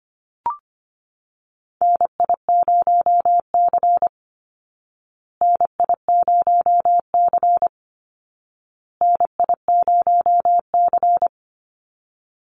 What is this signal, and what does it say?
Morse code, heard as a keyed tone, about 25 wpm, 700 Hz. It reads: NI0C NI0C NI0C
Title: Call Signs Repeated 2x - 25wpm